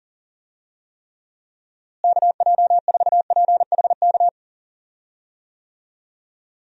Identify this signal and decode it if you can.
Morse code, heard as a keyed tone, about 40 wpm, 700 Hz. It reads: KJ4PHK
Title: USA Call Signs - 6 Characters - Repeated 3x - 40wpm